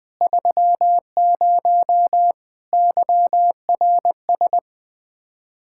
Morse code, heard as a keyed tone, about 20 wpm, 700 Hz. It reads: 30 YRS
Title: Speed-Racing Common QSO Phrases - 17wpm